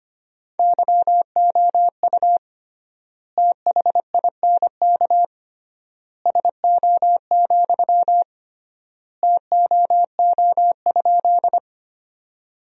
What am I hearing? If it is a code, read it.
Morse code, heard as a keyed tone, about 25 wpm, 700 Hz. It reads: YOU THINK SO, TOO?
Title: 3x Word-Spacing - Sentences from Top 200 Words - 25wpm